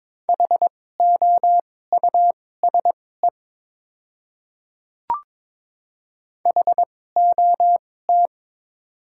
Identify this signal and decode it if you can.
Morse code, heard as a keyed tone, about 22 wpm, spaced as for 16 wpm, 700 Hz. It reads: HOUSE HOT
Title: 2x Character Spacing - Top 300 Words - 22wpm